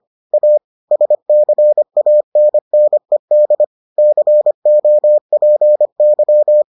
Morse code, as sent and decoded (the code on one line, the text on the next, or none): .- / ... -.-. .- -. -. -.. / -.-. --- .--. -.--
A SCANNED COPY